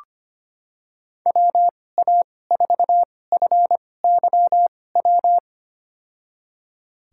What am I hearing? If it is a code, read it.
Morse code, heard as a keyed tone, about 25 wpm, spaced as for 18 wpm, 700 Hz. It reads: WA4FYW
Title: USA Call Signs - 6 Characters - 2x Character Spacing - 25wpm